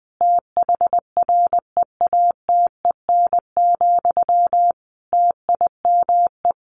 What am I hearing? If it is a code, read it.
Morse code, heard as a keyed tone, about 20 wpm, 700 Hz. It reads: THREATEN, TIME
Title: A Sound Of Thunder 20wpm